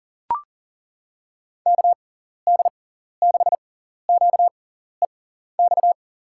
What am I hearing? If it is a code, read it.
Morse code, heard as a keyed tone, about 40 wpm, spaced as for 14 wpm, 700 Hz. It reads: KD6QEX
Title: USA Call Signs - 6x Character Spacing - 40wpm